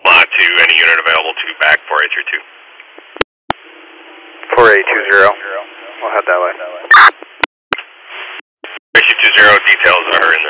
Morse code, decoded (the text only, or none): none